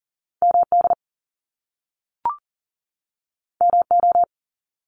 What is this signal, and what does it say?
Morse code, heard as a keyed tone, about 42 wpm, 700 Hz. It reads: MD MO
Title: US States 40wpm